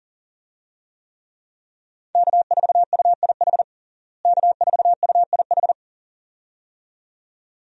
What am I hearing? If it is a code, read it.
Morse code, heard as a keyed tone, about 40 wpm, 700 Hz. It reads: K4UIH K4UIH
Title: USA Call Signs - 5 Characters - Repeated 2x - 40wpm